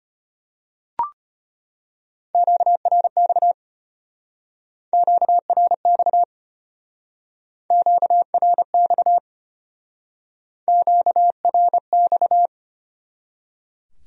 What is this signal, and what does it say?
Morse code, heard as a keyed tone, about 31 wpm, 700 Hz. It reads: QRX QRX QRX QRX
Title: Speed-Racing Q-Codes - 25wpm